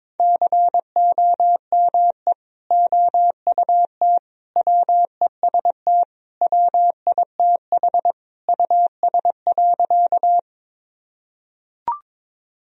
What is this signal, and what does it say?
Morse code, heard as a keyed tone, about 22 wpm, 700 Hz. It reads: COME OUT WEST WITH US.
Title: Sentences from Top 1000 Words - Review Part 2 - 22wpm